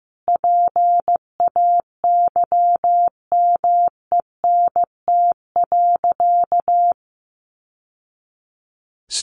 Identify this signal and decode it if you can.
Morse code, heard as a keyed tone, about 15 wpm, 700 Hz. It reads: PAYMENT.
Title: Sentences from Top 1200 Words 15wpm